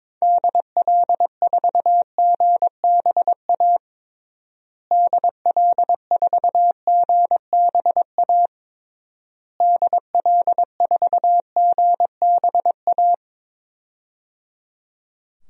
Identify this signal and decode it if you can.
Morse code, heard as a keyed tone, about 22 wpm, 700 Hz. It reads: DL4GBA DL4GBA DL4GBA
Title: DXCC-100 Call Signs - 6 Characters - Repeated 3x - 22wpm